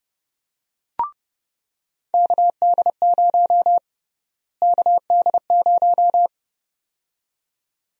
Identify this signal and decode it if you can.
Morse code, heard as a keyed tone, about 30 wpm, 700 Hz. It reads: KD0 KD0
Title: USA Call Sign Prefixes - Repeated 2x - 30wpm